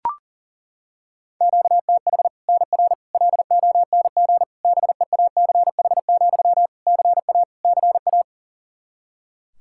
Morse code, heard as a keyed tone, about 37 wpm, 700 Hz. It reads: QTH NR LONG BEACH, CA CA
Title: K1USN SST QSO Practice 40wpm